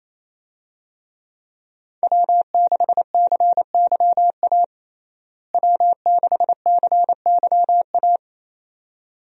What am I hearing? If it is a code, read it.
Morse code, heard as a keyed tone, about 28 wpm, 700 Hz. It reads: W6CYA W6CYA
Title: USA Call Signs - 5 Characters - Repeated 2x - 28wpm